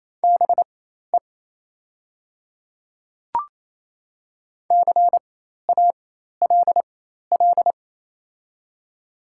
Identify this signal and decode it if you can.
Morse code, heard as a keyed tone, about 28 wpm, spaced as for 13 wpm, 700 Hz. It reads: BE CALL
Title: 4x Character Spacing - Top 100 Words - 28wpm